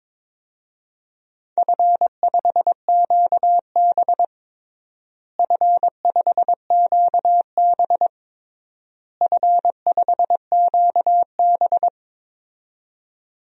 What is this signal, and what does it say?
Morse code, heard as a keyed tone, about 22 wpm, 700 Hz. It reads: F5QB F5QB F5QB
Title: DXCC-100 Call Signs - 4 Characters - Repeated 3x - 22wpm